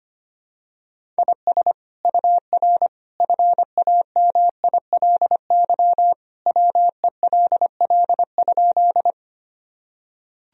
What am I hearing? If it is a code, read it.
Morse code, heard as a keyed tone, about 25 wpm, 700 Hz. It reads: IS UR FAMILY WELL?
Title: Sentences from Top 400 Words and Abbreviations 25wpm